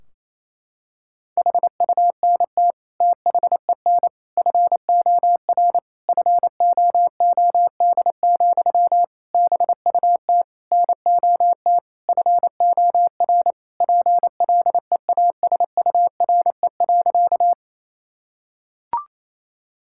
Morse code, heard as a keyed tone, about 28 wpm, 700 Hz. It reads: HUNT THEN FOR FOOD, BUT NOT FOR PLEASURE.
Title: Sentences from Top 1600 Words 28wpm